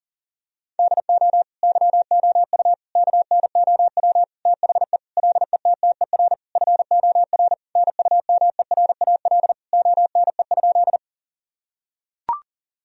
Morse code, heard as a keyed tone, about 40 wpm, 700 Hz. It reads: DO YOU KNOW THE LETTER FOR NUMERAL ONE?
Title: Sentences from Top 1000 Words - Review Part 2 - 40wpm